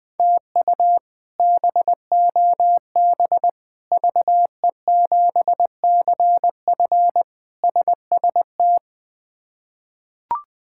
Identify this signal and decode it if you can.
Morse code, heard as a keyed tone, about 20 wpm, 700 Hz. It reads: TU BOB VE7CF SST
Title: K1USN SST QSO Practice - 20wpm